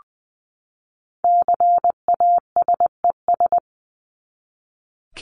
Morse code, heard as a keyed tone, about 20 wpm, 700 Hz. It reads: CASES
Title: ES:S Suffix - Top 100 Words with Base Words 20wpm